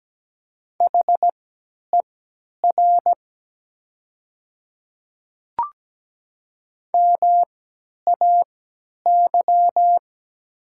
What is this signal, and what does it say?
Morse code, heard as a keyed tone, about 17 wpm, spaced as for 10 wpm, 700 Hz. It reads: HER MAY
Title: 3x Character Spacing - Top 500 Words - 17wpm